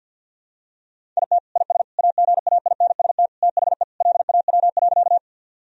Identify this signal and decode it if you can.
Morse code, heard as a keyed tone, about 50 wpm, 700 Hz. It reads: IT IS AGAINST THE LAW.
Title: Sentences from Top 1000 Words - Review Part 3 - 50wpm